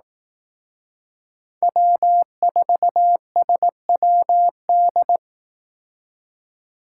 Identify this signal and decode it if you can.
Morse code, heard as a keyed tone, about 18 wpm, 700 Hz. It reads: W4SWD
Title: Speed Racing - USA Call Signs 15wpm